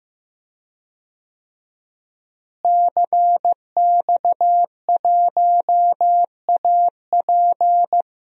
Morse code, heard as a keyed tone, about 15 wpm, 700 Hz. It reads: CX1AP
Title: DXCC-100 Call Signs - 5 Characters - Repeated 2x - 15wpm